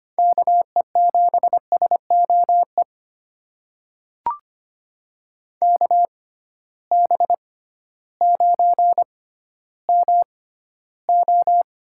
Morse code, heard as a keyed tone, about 25 wpm, spaced as for 13 wpm, 700 Hz. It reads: KE7SOE KB9MO
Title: USA Call Signs - 6 Characters - 6x Character Spacing - 25wpm